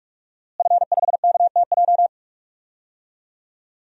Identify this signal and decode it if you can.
Morse code, heard as a keyed tone, about 44 wpm, 700 Hz. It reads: F5KTJ
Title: DXCC-100 Call Signs - 5 Characters - Repeated 3x - 45wpm